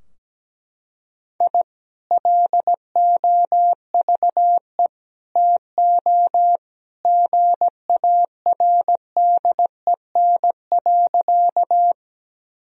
Morse code, heard as a keyed tone, about 17 wpm, 700 Hz. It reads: I LOVE TO GARDEN.
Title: Sentences from Top 1000 Words - Review Part 4 - 17wpm